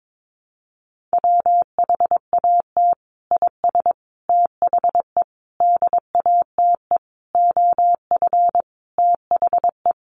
Morse code, heard as a keyed tone, about 22 wpm, 700 Hz. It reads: WHAT IS THE DATE OF THE
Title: Sentences from Top 1100 Words 22wpm